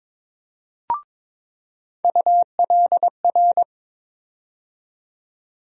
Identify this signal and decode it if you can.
Morse code, heard as a keyed tone, about 21 wpm, 700 Hz. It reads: ULR
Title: USA Call Sign Postfixes - 3 Letters 22wpm